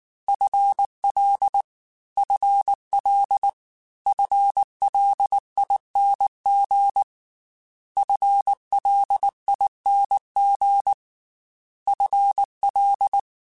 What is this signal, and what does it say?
Morse code, heard as a keyed tone, about 19 wpm, 800 Hz. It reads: FL FL FLING FLING FL
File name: FL / FL mp3s / FL Prefixes 3x2 @18